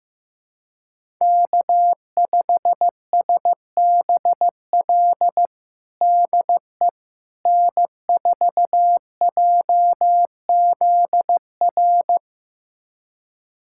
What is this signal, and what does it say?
Morse code, heard as a keyed tone, about 15 wpm, 700 Hz. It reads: K5SBL DE N4JZR
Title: USA Callsign DE Callsign - 5 Characters - 15wpm